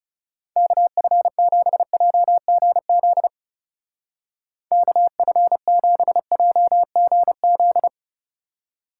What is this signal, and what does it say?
Morse code, heard as a keyed tone, about 37 wpm, 700 Hz. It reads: KF7JGZ KF7JGZ
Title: Speed Racing - USA 2x3 Call Signs 30wpm